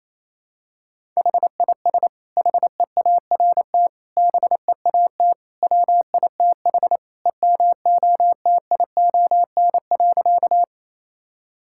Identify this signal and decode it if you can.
Morse code, heard as a keyed tone, about 27 wpm, 700 Hz. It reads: HIS HEART BEAT WITH EMOTION.
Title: Sentences from Top 1600 Words 28wpm